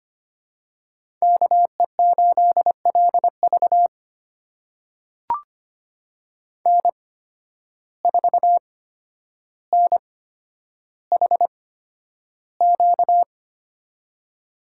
Morse code, heard as a keyed tone, about 25 wpm, spaced as for 11 wpm, 700 Hz. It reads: KE8LV N4NHQ
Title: USA Call Signs - 5 Characters - 8x Character Spacing - 25wpm